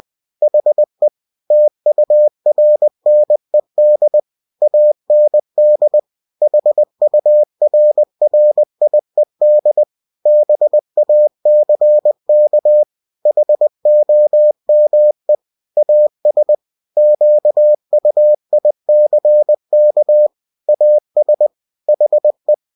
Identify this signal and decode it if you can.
Morse code, heard as a keyed tone, about 20 wpm, 600 Hz. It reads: HE TURNED AND HURRIED BACK HOME AS QUICK AS HE
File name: cw-books / winnie-the-pooh / winnie-20wpm-page-0048